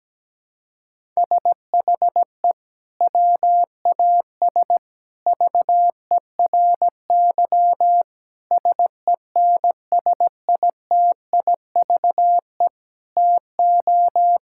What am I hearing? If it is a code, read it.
Morse code, heard as a keyed tone, about 17 wpm, 700 Hz. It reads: SHE WAS VERY SENSITIVE TO
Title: Sentences from Top 1600 Words 17wpm